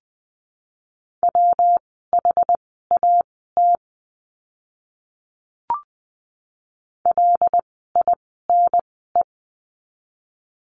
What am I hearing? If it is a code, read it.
Morse code, heard as a keyed tone, about 20 wpm, spaced as for 15 wpm, 700 Hz. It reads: WHAT LINE
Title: 115 Lesson 19f - Letter w - Words 20wpm